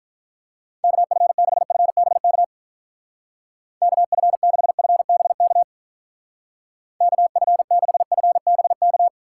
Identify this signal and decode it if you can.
Morse code, heard as a keyed tone, about 42 wpm, 700 Hz. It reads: KF6FBK KF6FBK KF6FBK
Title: Speed Racing - USA 2x3 Call Signs 35wpm